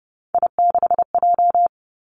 Morse code, heard as a keyed tone, about 30 wpm, 700 Hz. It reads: I6J
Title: Three Challenging Letter-Number Combinations 30wpm